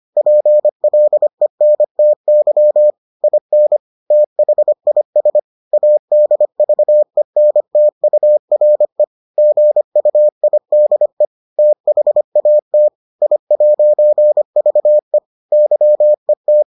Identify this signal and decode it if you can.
Morse code, heard as a keyed tone, about 25 wpm, 600 Hz. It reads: PLENTY IN THIS ADVENTURE GUIDE THAT I'VE YET